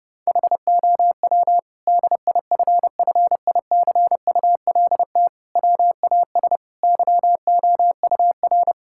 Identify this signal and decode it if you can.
Morse code, heard as a keyed tone, about 30 wpm, 700 Hz. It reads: HOW DIFFICULT WAS YOUR
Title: Sentences from Top 1000 Words - Review Part 6 - 30wpm